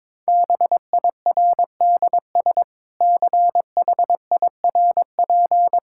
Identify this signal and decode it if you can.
Morse code, heard as a keyed tone, about 22 wpm, 700 Hz. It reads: BIRDS CHIRP
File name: Difficult Sentences with Hints - 22wpm